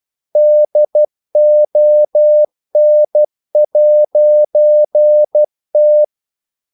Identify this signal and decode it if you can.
Morse code, heard as a keyed tone, about 12 wpm, 600 Hz. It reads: DON'T
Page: English quotations in CW (telegraphy - morse)